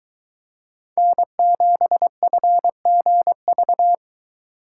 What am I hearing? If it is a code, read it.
Morse code, heard as a keyed tone, about 23 wpm, 700 Hz. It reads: N7FGV